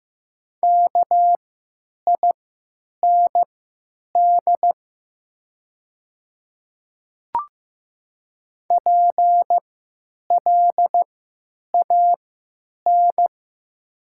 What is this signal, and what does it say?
Morse code, heard as a keyed tone, about 15 wpm, spaced as for 8 wpm, 700 Hz. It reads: KIND PLAN
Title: 3x Character Spacing - Top 500 Words - 15wpm